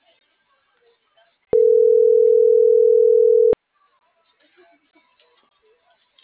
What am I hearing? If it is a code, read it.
Ringback tone (North America), 440 + 480 Hz.